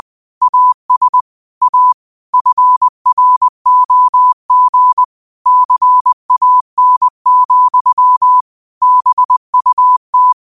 Morse code, heard as a keyed tone, about 20 wpm, 1000 Hz. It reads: AS A FROG CAN, BUT
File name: qotdfiles / 20wpm